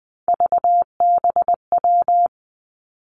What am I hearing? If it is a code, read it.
Morse code, heard as a keyed tone, about 20 wpm, 700 Hz. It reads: VBW
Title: Three Challenging Letter-Number Combinations 20wpm